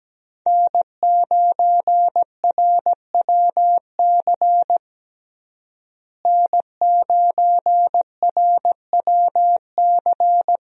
Morse code, heard as a keyed tone, about 17 wpm, 700 Hz. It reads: N9RWC N9RWC
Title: USA Call Signs - 5 Characters - Repeated 3x - 17wpm